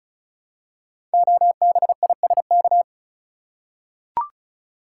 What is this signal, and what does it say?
Morse code, heard as a keyed tone, about 36 wpm, 700 Hz. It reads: OBISK
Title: Code Groups - 5 Letters - 35wpm